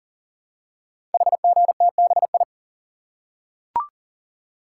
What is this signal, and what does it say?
Morse code, heard as a keyed tone, about 42 wpm, 690 Hz. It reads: HGTBI
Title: Code Groups - 5 Letters - 40wpm